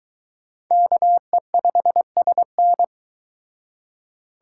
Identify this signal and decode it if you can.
Morse code, heard as a keyed tone, about 23 wpm, 700 Hz. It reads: KE5SN